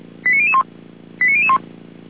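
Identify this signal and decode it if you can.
Five-tone selective call (ZVEI), heard as quick runs of five tones, tones of about 70 ms. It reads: 89001 89001
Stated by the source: ZVEI five-tone call 89010 received on an SDR, from audio.f85360.feuerwehr-orlamuende.org (Feuerwehr Orlamünde fire service).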